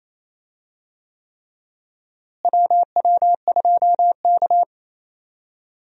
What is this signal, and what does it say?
Morse code, heard as a keyed tone, about 28 wpm, 700 Hz. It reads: WW2K
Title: USA Call Signs - 4 Characters - Repeated 3x - 28wpm